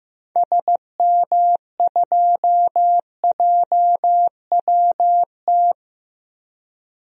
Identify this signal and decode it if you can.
Morse code, heard as a keyed tone, about 15 wpm, 700 Hz. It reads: SM2JWT
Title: DXCC-100 Call Signs - 6 Characters - Repeated 2x - 15wpm